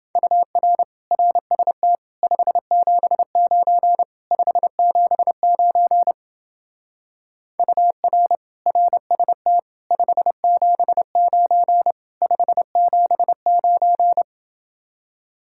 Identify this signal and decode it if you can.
Morse code, heard as a keyed tone, about 29 wpm, 700 Hz. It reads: UR RST 579 579 UR RST 579 579